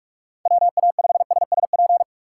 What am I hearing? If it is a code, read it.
Morse code, heard as a keyed tone, about 44 wpm, 700 Hz. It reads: WA5SSP
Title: USA Call Signs - 6 Characters - 6x Character Spacing - 45wpm